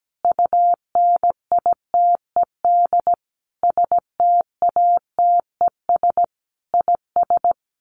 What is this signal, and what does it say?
Morse code, heard as a keyed tone, about 17 wpm, 700 Hz. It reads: UNITED STATES IS
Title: Sets of 5 Words Encore 17wpm